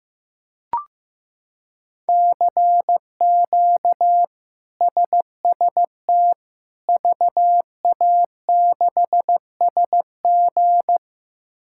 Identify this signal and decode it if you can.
Morse code, heard as a keyed tone, about 15 wpm, 700 Hz. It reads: CQ SST VA6SG
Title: K1USN SST QSO Practice - 15wpm